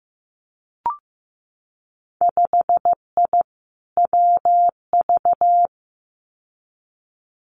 Morse code, heard as a keyed tone, about 15 wpm, 700 Hz. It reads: 5I WV
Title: Winter Field Day Exchanges Encore 15wpm